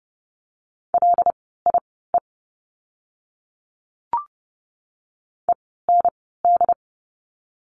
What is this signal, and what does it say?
Morse code, heard as a keyed tone, about 30 wpm, spaced as for 17 wpm, 700 Hz. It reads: LIE END
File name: Lesson 16f - Letter c - Words 30wpm